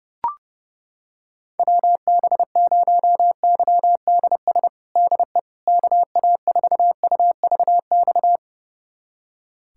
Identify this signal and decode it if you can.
Morse code, heard as a keyed tone, about 32 wpm, 700 Hz. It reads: WB0YDS DE KA4UVX
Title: USA Callsign DE Callsign - 6 Characters - 30wpm